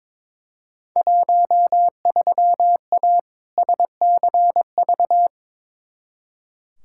Morse code, heard as a keyed tone, about 22 wpm, 700 Hz. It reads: 13A SCV
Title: Field Day Exchanges Encore - 22wpm